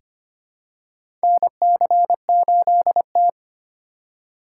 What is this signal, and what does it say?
Morse code, heard as a keyed tone, about 25 wpm, 700 Hz. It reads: NC8T
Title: USA Call Signs - 4 Characters - 3x Character Spacing - 25wpm